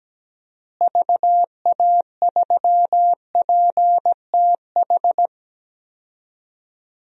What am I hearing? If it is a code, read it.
Morse code, heard as a keyed tone, about 17 wpm, 700 Hz. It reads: VA3PTH
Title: DXCC-100 Call Signs - 6 Characters - Repeated 2x - 17wpm